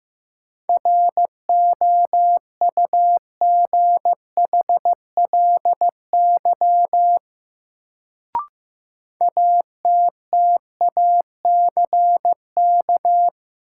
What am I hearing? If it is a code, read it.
Morse code, heard as a keyed tone, about 15 wpm, 700 Hz. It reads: ROUGHLY ATTACK
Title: Sentences from Top 1500 Words - New Word Drill Alt - 15wpm